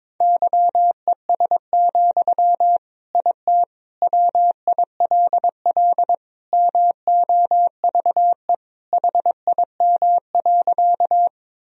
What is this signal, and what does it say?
Morse code, heard as a keyed tone, about 22 wpm, 700 Hz. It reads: YES, IT WILL MOVE HIM.
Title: Sentences from Top 1000 Words - Review Part 1 - 22wpm